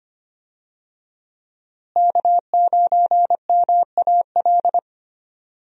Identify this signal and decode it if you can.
Morse code, heard as a keyed tone, about 25 wpm, 700 Hz. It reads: K9MAL
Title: USA Call Signs - 5 Characters - Repeated 3x - 25wpm